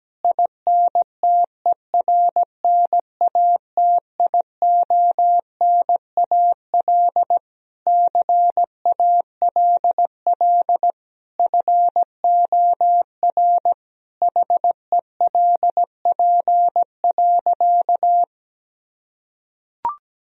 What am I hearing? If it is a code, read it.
Morse code, heard as a keyed tone, about 17 wpm, 700 Hz. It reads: INTERNATIONAL CALL FOR HELP.
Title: Sentences from Top 1000 Words - Review Part 4 - 17wpm